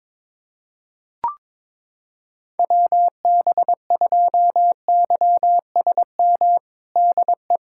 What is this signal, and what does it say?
Morse code, heard as a keyed tone, about 22 wpm, 700 Hz. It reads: WB2YSM DE
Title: USA Callsign DE Callsign 22wpm